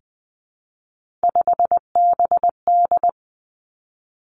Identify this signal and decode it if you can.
Morse code, heard as a keyed tone, about 20 wpm, 700 Hz. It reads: 5BD
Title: Three Challenging Letter-Number Combinations 20wpm